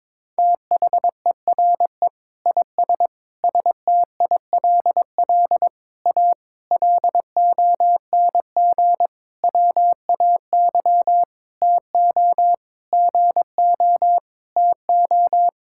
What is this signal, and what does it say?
Morse code, heard as a keyed tone, about 22 wpm, 700 Hz. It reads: THERE IS STILL A LONG WAY TO GO TO
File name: practice / 22wpm / Sentences with 8-Word Phrases - 22wpm